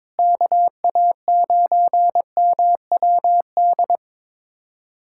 Morse code, heard as a keyed tone, about 22 wpm, 700 Hz. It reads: KA9MWD
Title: USA Callsign DE Callsign - 6 Characters - 22wpm